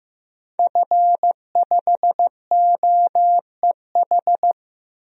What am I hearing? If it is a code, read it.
Morse code, heard as a keyed tone, about 15 wpm, 700 Hz. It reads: F5OEH
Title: DXCC-100 Call Signs - 5 Characters - 15wpm